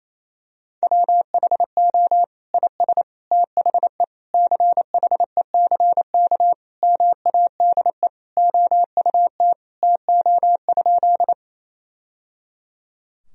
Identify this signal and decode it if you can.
Morse code, heard as a keyed tone, about 28 wpm, 700 Hz. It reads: WHO IS THE CHECK MADE OUT TO?
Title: Sentences from Top 1000 Words - Review Part 3 - 28wpm